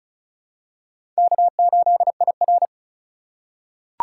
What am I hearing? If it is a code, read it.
Morse code, heard as a keyed tone, about 34 wpm, 700 Hz. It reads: K8IR
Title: USA Call Signs - 4 Characters - 2x Character Spacing - 35wpm